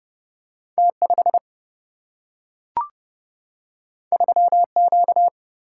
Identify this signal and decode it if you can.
Morse code, heard as a keyed tone, about 30 wpm, 700 Hz. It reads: T 5 3 Q
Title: Single-Letter Single-Number 30wpm